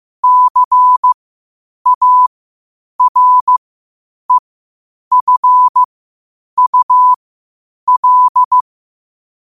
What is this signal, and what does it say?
Morse code, heard as a keyed tone, about 15 wpm, spaced as for 8 wpm, 1000 Hz. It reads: CAREFUL
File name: qotdfiles / 8wpm